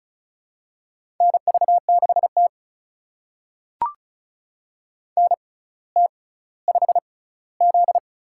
Morse code, heard as a keyed tone, about 36 wpm, spaced as for 18 wpm, 700 Hz. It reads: NV6T NT5Z
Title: USA Call Signs - 4 Characters - 6x Character Spacing - 35wpm